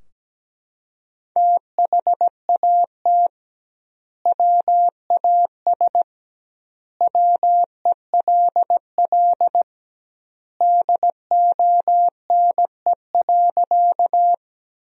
Morse code, heard as a keyed tone, about 17 wpm, 700 Hz. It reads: THAT WAS WELL DONE.
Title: 2x Word-Spacing - Sentences from Top 500 Words - 17wpm